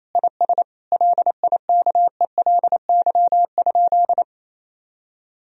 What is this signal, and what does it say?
Morse code, heard as a keyed tone, about 28 wpm, 700 Hz. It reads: IS LIKELY?
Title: Sentences from Top 1000 Words - Review Part 4 - 28wpm